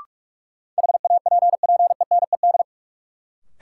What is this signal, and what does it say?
Morse code, heard as a keyed tone, about 42 wpm, 700 Hz. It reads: HAPPENED
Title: Sentences from Top 1000 Words - Review Part 2 - New Word Drill - 45wpm